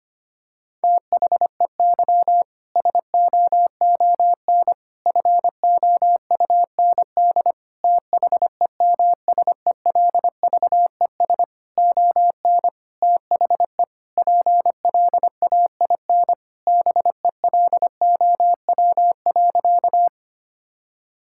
Morse code, heard as a keyed tone, about 25 wpm, 700 Hz. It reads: THEY SOON FOUND THEMSELVES ON THE PLAIN BELOW.
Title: Sentences from Top 1000 Words - Review Part 5 - 25wpm